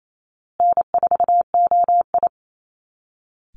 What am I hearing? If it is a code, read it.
Morse code, heard as a keyed tone, about 28 wpm, 700 Hz. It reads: N4OI
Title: Easy Call Signs Repeated 3x - 28wpm